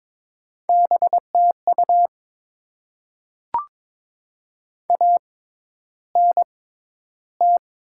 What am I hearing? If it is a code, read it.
Morse code, heard as a keyed tone, about 22 wpm, spaced as for 11 wpm, 700 Hz. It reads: BTU ANT
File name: Top 20 QSO Elements - 6x Character Spacing - 22wpm